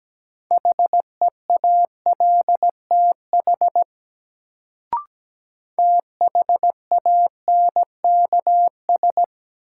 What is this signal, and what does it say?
Morse code, heard as a keyed tone, about 17 wpm, 700 Hz. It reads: HEALTH THANKS
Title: Sets of 5 Words - New Word Drill - 17wpm